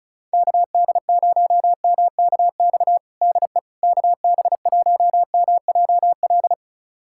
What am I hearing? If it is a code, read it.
Morse code, heard as a keyed tone, about 34 wpm, 690 Hz. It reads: KD0MKX DE KB1MJL
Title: USA Callsign DE Callsign - 6 Characters - 35wpm